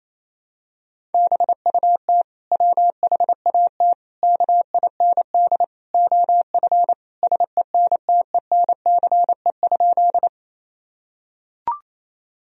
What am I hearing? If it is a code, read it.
Morse code, heard as a keyed tone, about 28 wpm, 700 Hz. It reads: BUT WHAT KIND OF SENTENCE?
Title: Sentences from Top 1000 Words - Review Part 1 - 28wpm